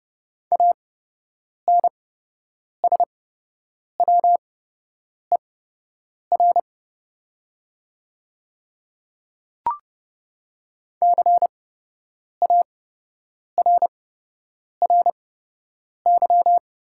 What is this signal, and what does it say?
Morse code, heard as a keyed tone, about 30 wpm, spaced as for 8 wpm, 690 Hz. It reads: ANSWER CARRY
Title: 8x Character Spacing - Top 300 Words - 30wpm